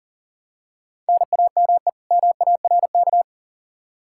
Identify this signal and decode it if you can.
Morse code, heard as a keyed tone, about 40 wpm, 700 Hz. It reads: NAME MARK
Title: Practice QSO - Exchange 1 - Beginning - 40wpm